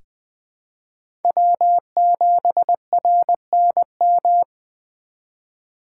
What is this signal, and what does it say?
Morse code, heard as a keyed tone, about 21 wpm, 700 Hz. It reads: W7RNM
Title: USA Call Signs - 5 Characters - 2x Character Spacing - 20wpm